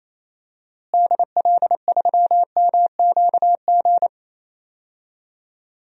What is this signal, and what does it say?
Morse code, heard as a keyed tone, about 28 wpm, 700 Hz. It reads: DL3MQG
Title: DXCC-100 Call Signs - 6 Characters - Repeated 3x - 28wpm